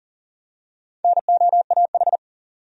Morse code, heard as a keyed tone, about 40 wpm, 700 Hz. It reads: NOAH
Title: USA Common Names - Max 5 Letters - 40wpm